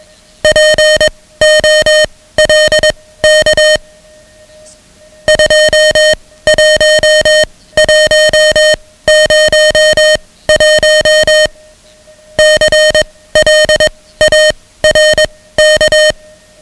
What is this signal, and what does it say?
Morse code, heard as a keyed tone, about 21 wpm, spaced as for 15 wpm, 620 Hz, with noise. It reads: POLK 21101 CLARK
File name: ham radio / CodeQuick / Code Quick 10